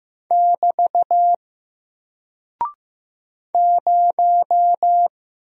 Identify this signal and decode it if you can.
Morse code, heard as a keyed tone, about 15 wpm, 700 Hz. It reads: <BT> 0
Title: Single Character (Letter:Number:Punctuation:Prosign) 15wpm